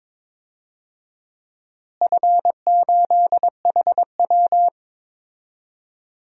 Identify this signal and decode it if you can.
Morse code, heard as a keyed tone, about 21 wpm, 700 Hz. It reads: F8HW
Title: DXCC-100 Call Signs - 4 Characters - Repeated 3x - 22wpm